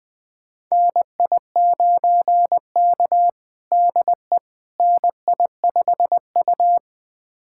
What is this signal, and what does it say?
Morse code, heard as a keyed tone, about 20 wpm, 700 Hz. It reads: NI9K DE NI5U
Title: USA Callsign DE Callsign - 4 Characters - 20wpm